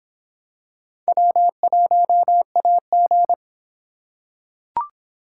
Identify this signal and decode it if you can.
Morse code, heard as a keyed tone, about 23 wpm, 700 Hz. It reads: W1AG